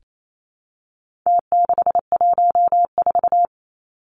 Morse code, heard as a keyed tone, about 28 wpm, 700 Hz. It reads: T614
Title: Numbers - Contest Style - 4-Digit 28wpm